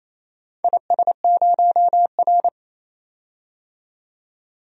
Morse code, heard as a keyed tone, about 27 wpm, 690 Hz. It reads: IS0R
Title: DXCC-100 Call Signs - 4 Characters - 28wpm